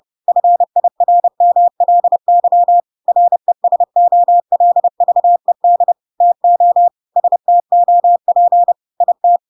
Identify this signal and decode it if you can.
Morse code, heard as a keyed tone, about 30 wpm, 700 Hz. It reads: FIRMLY RESOLVED TO STOP IT
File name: Pinocchio / Pinocchio ch3 30 30 ik1hge